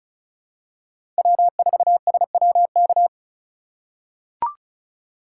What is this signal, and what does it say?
Morse code, heard as a keyed tone, about 36 wpm, 700 Hz. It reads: W4SWK
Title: USA Call Signs - 5 Characters - 2x Character Spacing - 35wpm